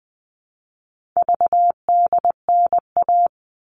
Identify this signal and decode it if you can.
Morse code, heard as a keyed tone, about 20 wpm, 700 Hz. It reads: VDNA
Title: All Cut Numbers - 4-Digit 20wpm